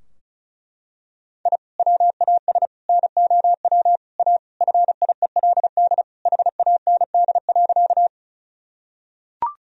Morse code, heard as a keyed tone, about 34 wpm, 700 Hz. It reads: I WAS NOW A FIELD HAND.
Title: Sentences from Top 1000 Words - Review Part 2 - 35wpm